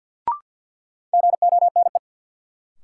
Morse code, heard as a keyed tone, about 53 wpm, 700 Hz. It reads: GONE
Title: Sentences from Top 1200 Words - New Word Drill Alt - 50wpm